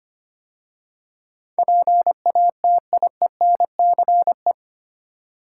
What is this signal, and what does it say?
Morse code, heard as a keyed tone, about 25 wpm, 700 Hz. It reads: PATIENCE
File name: 8 Letter Words - First 3 Letters Given - 25wpm